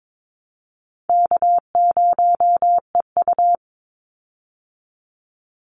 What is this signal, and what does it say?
Morse code, heard as a keyed tone, about 22 wpm, 700 Hz. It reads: K0EU
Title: Call Signs Repeated 2x - 22wpm